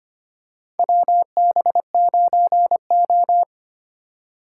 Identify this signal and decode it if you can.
Morse code, heard as a keyed tone, about 25 wpm, 700 Hz. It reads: WB9O
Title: USA Call Signs - 3x Character Spacing - 25wpm